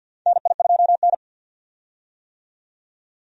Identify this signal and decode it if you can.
Morse code, heard as a keyed tone, about 48 wpm, 700 Hz. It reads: NI2N
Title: USA Call Signs - 4 Characters - 50wpm